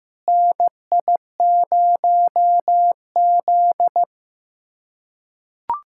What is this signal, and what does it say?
Morse code, heard as a keyed tone, about 16 wpm, 700 Hz. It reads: NI0Z
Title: USA Callsign DE Callsign - 4 Characters - 15wpm